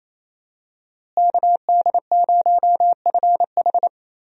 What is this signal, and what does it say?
Morse code, heard as a keyed tone, about 28 wpm, 700 Hz. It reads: KD0FH
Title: USA Call Signs - 5 Characters - Repeated 2x - 28wpm